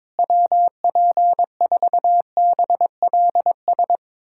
Morse code, heard as a keyed tone, about 22 wpm, 700 Hz. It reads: WP4BLS
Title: USA Callsign DE Callsign - 6 Characters - 22wpm